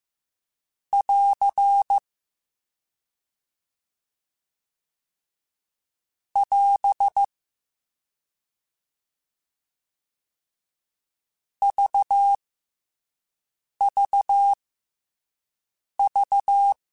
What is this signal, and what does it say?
Morse code, heard as a keyed tone, about 15 wpm, spaced as for 5 wpm, 800 Hz. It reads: <AR> <AS> VVV
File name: F and L / F and L mp3 / F and L Prefixes 3x2 Twice @5